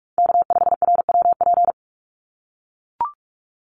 Morse code, heard as a keyed tone, about 44 wpm, 700 Hz. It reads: K5RWP
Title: NA SOTA Chaser Callsigns 45wpm